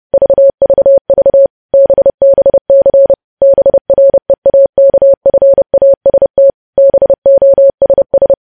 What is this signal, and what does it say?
Morse code, heard as a keyed tone, about 30 wpm, 550 Hz. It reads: VVV BBC BREAKFAST BOSS
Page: news headlines in morse code at 30 wpm